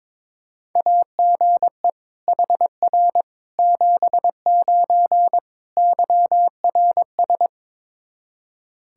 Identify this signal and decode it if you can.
Morse code, heard as a keyed tone, about 22 wpm, 700 Hz. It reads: AGE HR 79 YRS
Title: K1USN SST QSO Practice 22wpm